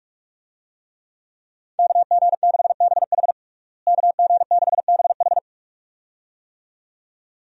Morse code, heard as a keyed tone, about 41 wpm, 700 Hz. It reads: KG6BH KG6BH
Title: USA Call Signs - 5 Characters - Repeated 2x - 45wpm